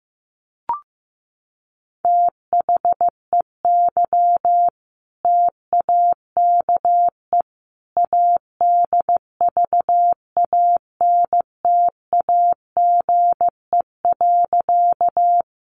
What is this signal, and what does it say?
Morse code, heard as a keyed tone, about 14 wpm, 700 Hz. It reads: THEY TAKE ADVANTAGE.
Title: Sentences from Top 900 Words 15wpm